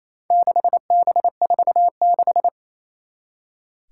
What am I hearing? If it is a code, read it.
Morse code, heard as a keyed tone, about 27 wpm, 700 Hz. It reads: 6B46
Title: Four Challenging Letter-Number Combinations 28wpm